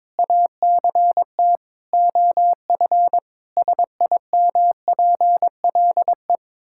Morse code, heard as a keyed tone, about 22 wpm, 700 Hz. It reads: ACT OF SIMPLE
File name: Difficult Sentences with Hints - 22wpm